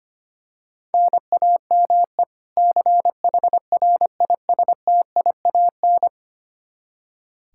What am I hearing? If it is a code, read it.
Morse code, heard as a keyed tone, about 25 wpm, 700 Hz. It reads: NAME CHRISTIAN
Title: Practice QSO - Exchange 1 - Beginning - 25wpm